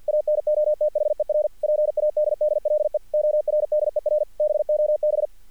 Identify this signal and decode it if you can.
Morse code, heard as a keyed tone, about 46 wpm, 600 Hz. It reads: ANOTHER PADDLE ORDER BOB
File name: archived / CRSnet